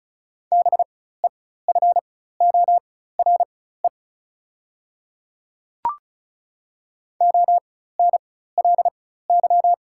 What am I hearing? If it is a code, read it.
Morse code, heard as a keyed tone, about 36 wpm, spaced as for 16 wpm, 700 Hz. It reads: BEFORE ONLY
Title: 4x Character Spacing - Top 300 Words - 35wpm